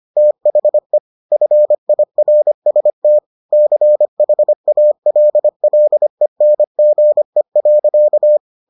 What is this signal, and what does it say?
Morse code, heard as a keyed tone, about 25 wpm, 600 Hz. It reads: THE FIRST CHALLENGE.